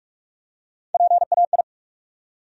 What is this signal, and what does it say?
Morse code, heard as a keyed tone, about 46 wpm, 700 Hz. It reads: PAI
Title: USA Call Sign Postfixes - 3 Letters 45wpm